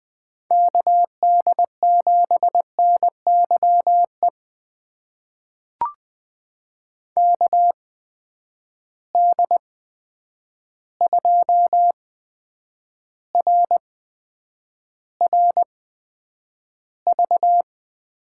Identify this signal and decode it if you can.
Morse code, heard as a keyed tone, about 19 wpm, spaced as for 9 wpm, 700 Hz. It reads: KD7NYE KD2RRV
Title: USA Call Signs - 6 Characters - 8x Character Spacing - 20wpm